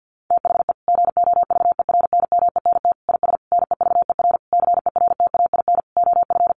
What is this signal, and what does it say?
Morse code, heard as a keyed tone, about 49 wpm, 700 Hz. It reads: THE GOVERNMENT IS NEVER CERTAIN OF
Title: Sentences from Top 1300 Words 50wpm